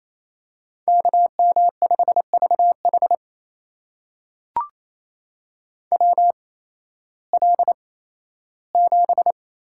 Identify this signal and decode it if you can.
Morse code, heard as a keyed tone, about 28 wpm, spaced as for 15 wpm, 700 Hz. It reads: KM5VH WL7
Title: USA Call Signs - 5 Characters - 8x Character Spacing - 28wpm